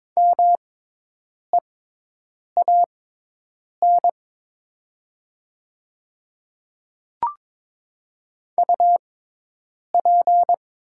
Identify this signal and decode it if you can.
Morse code, heard as a keyed tone, about 22 wpm, spaced as for 8 wpm, 700 Hz. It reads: MEAN UP
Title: 6x Character Spacing - Top 200 Words - 22wpm